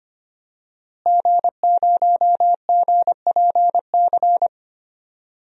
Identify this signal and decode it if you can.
Morse code, heard as a keyed tone, about 25 wpm, 700 Hz. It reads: G0GPC
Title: DXCC-100 Call Signs - 5 Characters - Repeated 3x - 25wpm